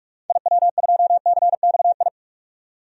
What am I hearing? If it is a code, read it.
Morse code, heard as a keyed tone, about 46 wpm, 700 Hz. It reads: IW2CXI
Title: DXCC-100 Call Signs - 6 Characters - 45wpm